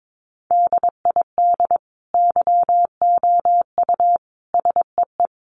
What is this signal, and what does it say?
Morse code, heard as a keyed tone, about 22 wpm, 700 Hz. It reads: DID YOU SEE
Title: Sentences from Top 1500 Words 22wpm